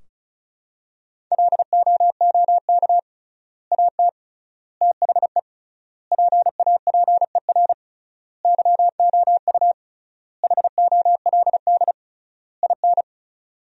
Morse code, heard as a keyed tone, about 36 wpm, 700 Hz. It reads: LOOK AT THE PAPER YOU HOLD IN
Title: 3x Word-Spacing - Sentences from Top 400 Words - 35wpm